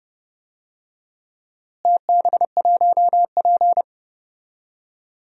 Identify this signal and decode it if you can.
Morse code, heard as a keyed tone, about 30 wpm, 700 Hz. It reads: TB1P
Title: DXCC-100 Call Signs - 4 Characters - Repeated 3x - 30wpm